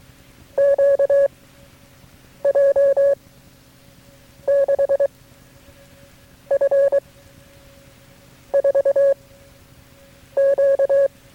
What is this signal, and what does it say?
Morse code, heard as a keyed tone, about 23 wpm, 560 Hz. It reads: Q J 6 F 4 Q
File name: FreeFiles / CW-practice / KB6MT-tape3-side1